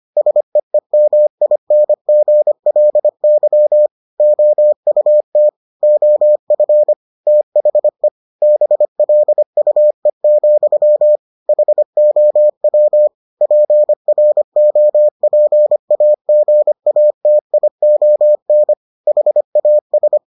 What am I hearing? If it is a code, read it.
Morse code, heard as a keyed tone, about 25 wpm, 600 Hz. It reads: SEEMINGLY OUT OF THE BLUE, HOW PROPAGATION HAS